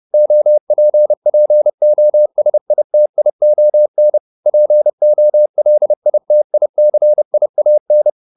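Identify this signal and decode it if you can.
Morse code, heard as a keyed tone, about 30 wpm, 600 Hz. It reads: OPPOSITION POLITICIAN
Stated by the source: Morse code at 30 wpm, the novel Never by Ken Follett in English, from CW med Gnister.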